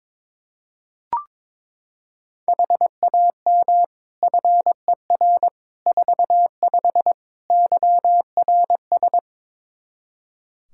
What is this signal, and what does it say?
Morse code, heard as a keyed tone, about 22 wpm, 700 Hz. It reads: HAM FER 45 YRS
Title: Practice QSO - Exchange 3 - More Info - 22wpm